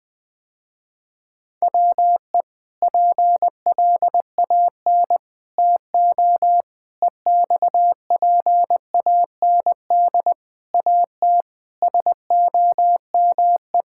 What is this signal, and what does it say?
Morse code, heard as a keyed tone, about 20 wpm, 700 Hz. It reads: WE PLAN TO EXPAND AT SOME
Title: Sentences with 6-Word Phrases - Spoken Beginning - 20wpm